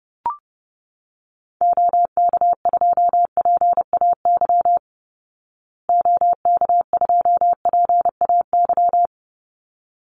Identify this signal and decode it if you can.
Morse code, heard as a keyed tone, about 30 wpm, 700 Hz. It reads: OK2PAY OK2PAY
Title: Call Signs Repeated 3x - 30wpm